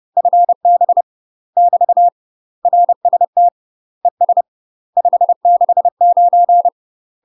Morse code, heard as a keyed tone, about 30 wpm, 700 Hz. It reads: FB <BT> RST ES 569
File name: UK MASTER QSOs TRAINING / 30WPM / UK MASTER QSOs 0042